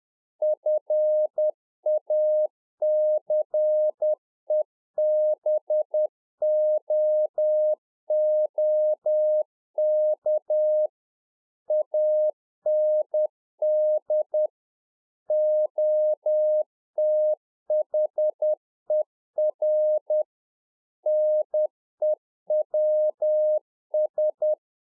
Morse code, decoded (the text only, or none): FACEBOOK AND OTHER NEWS